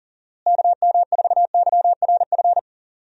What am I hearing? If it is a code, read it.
Morse code, heard as a keyed tone, about 40 wpm, 700 Hz. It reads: KM4YRF
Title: USA Call Signs - 6 Characters - Repeated 3x - 40wpm